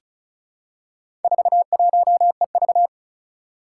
Morse code, heard as a keyed tone, about 34 wpm, 700 Hz. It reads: 41EV
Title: Four Challenging Letter-Number Combinations 35wpm